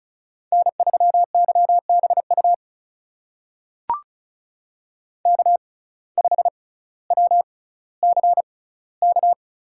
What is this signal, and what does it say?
Morse code, heard as a keyed tone, about 36 wpm, spaced as for 18 wpm, 700 Hz. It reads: N3YBU K5WCK
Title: USA Call Signs - 5 Characters - 6x Character Spacing - 35wpm